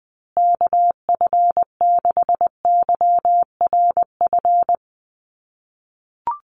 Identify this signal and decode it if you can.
Morse code, heard as a keyed tone, about 20 wpm, 700 Hz. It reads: KF6YRF
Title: USA Call Signs Repeated 2x 20wpm